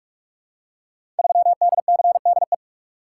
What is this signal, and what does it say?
Morse code, heard as a keyed tone, about 44 wpm, 700 Hz. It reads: 3DCDE
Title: Code Groups - 5 Letter:Numbers - 45wpm